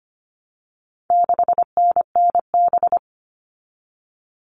Morse code, heard as a keyed tone, about 25 wpm, 700 Hz. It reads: 6NNB
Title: All Cut Numbers - 4-Digit 25wpm